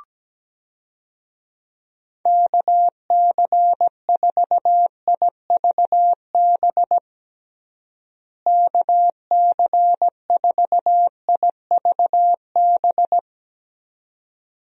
Morse code, heard as a keyed tone, about 17 wpm, 700 Hz. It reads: KC4IVB KC4IVB
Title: USA Call Signs - 6 Characters - Repeated 3x - 17wpm